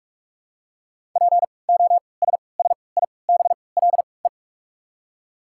Morse code, heard as a keyed tone, about 46 wpm, spaced as for 26 wpm, 700 Hz. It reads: POSSIBLE
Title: 3x Character Spacing - Top 500 Words - 45wpm